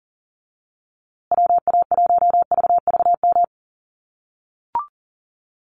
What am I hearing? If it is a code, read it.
Morse code, heard as a keyed tone, about 40 wpm, 700 Hz. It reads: WA1VVM
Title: USA Call Signs Repeated 2x 40wpm